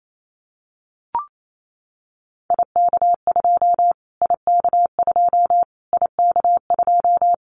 Morse code, heard as a keyed tone, about 28 wpm, 700 Hz. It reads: IK2 IK2 IK2
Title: Callsign Prefixes for Top 25 DXCC - Repeated 3x - 28wpm